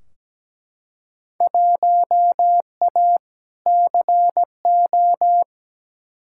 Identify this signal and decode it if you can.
Morse code, heard as a keyed tone, about 17 wpm, 700 Hz. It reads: 1A CO
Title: Field Day Exchanges - 17wpm